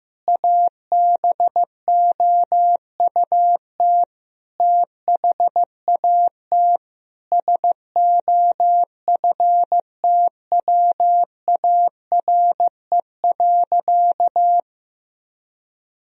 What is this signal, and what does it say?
Morse code, heard as a keyed tone, about 15 wpm, 700 Hz. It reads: ABOUT THAT SOFTWARE.
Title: Sentences from Top 1000 Words - Review Part 3 - 15wpm